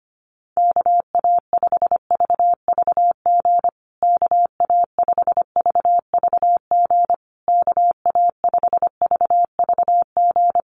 Morse code, heard as a keyed tone, about 25 wpm, 700 Hz. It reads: KA5VVG KA5VVG KA5VVG
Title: USA Call Signs Repeated 3x 25wpm